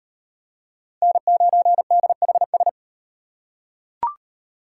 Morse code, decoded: N9DHS

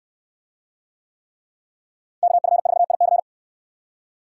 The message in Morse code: KF4IC

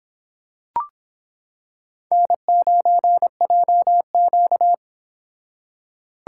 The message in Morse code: N9JQ

